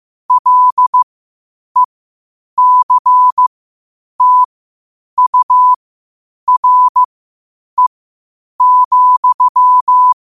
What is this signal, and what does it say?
Morse code, heard as a keyed tone, about 15 wpm, spaced as for 8 wpm, 1000 Hz. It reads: LECTURE,
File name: qotdfiles / 8wpm